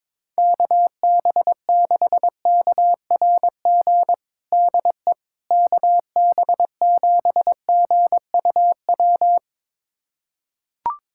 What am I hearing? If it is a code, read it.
Morse code, heard as a keyed tone, about 21 wpm, 700 Hz. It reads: KB6KRG DE KB7GUW
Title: USA Callsign DE Callsign - 6 Characters - 22wpm